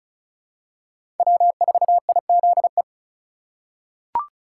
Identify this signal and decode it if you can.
Morse code, heard as a keyed tone, about 36 wpm, 700 Hz. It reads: W4IZE
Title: USA Call Signs - 3x Character Spacing - 35wpm